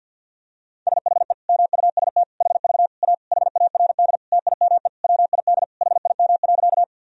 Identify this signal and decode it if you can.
Morse code, heard as a keyed tone, about 51 wpm, 700 Hz. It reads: SHE MUST HV A HARD TIME WID HIM.
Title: Sentences from Top 300 Words and Abbreviations 50wpm